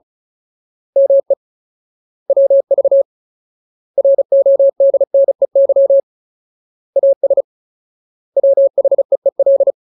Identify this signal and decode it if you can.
Morse code, heard as a keyed tone, about 34 wpm, 550 Hz. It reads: ME WV RODNEY AS WHEEL
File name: Ultimate Word List / 35WPM Ultimate